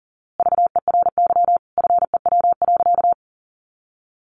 Morse code, heard as a keyed tone, about 40 wpm, 700 Hz. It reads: VERY FEW.
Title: A Sound Of Thunder 40wpm